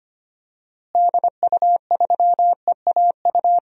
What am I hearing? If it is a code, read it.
Morse code, heard as a keyed tone, about 25 wpm, 700 Hz. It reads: DU3EAU
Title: DXCC-100 Call Signs - 6 Characters - Repeated 2x - 25wpm